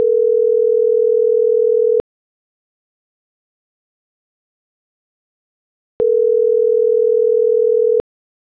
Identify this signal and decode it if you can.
Ringback tone (North America), 440 + 480 Hz.